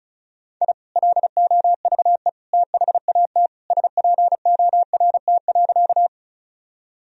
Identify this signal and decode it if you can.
Morse code, heard as a keyed tone, about 34 wpm, 700 Hz. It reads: I LOVE THAT SPORT.
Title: Sentences from Top 1000 Words - Review Part 4 - 35wpm